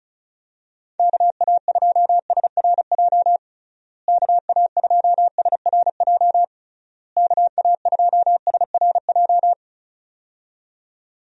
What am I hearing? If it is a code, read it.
Morse code, heard as a keyed tone, about 34 wpm, 700 Hz. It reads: KA2SRJ KA2SRJ KA2SRJ
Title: USA Call Signs - 6 Characters - Repeated 3x - 35wpm